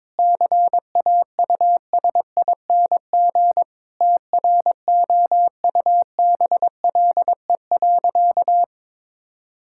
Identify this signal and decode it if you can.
Morse code, heard as a keyed tone, about 22 wpm, 700 Hz. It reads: CAUSING TROUBLE.
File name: Sentences with 9-Word Phrases - 22wpm